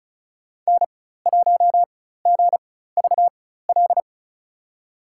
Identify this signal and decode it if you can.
Morse code, heard as a keyed tone, about 34 wpm, spaced as for 16 wpm, 700 Hz. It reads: N1GVL
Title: USA Call Signs - 5 Characters - 4x Character Spacing - 35wpm